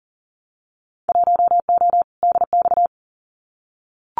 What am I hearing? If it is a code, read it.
Morse code, heard as a keyed tone, about 40 wpm, 700 Hz. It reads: 1O DX
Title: Winter Field Day Exchanges 40wpm